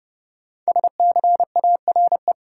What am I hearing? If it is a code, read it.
Morse code, heard as a keyed tone, about 29 wpm, 700 Hz. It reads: SCARE
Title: Sentences from Top 1000 Words - Review Part 5 - New Word Drill - 30wpm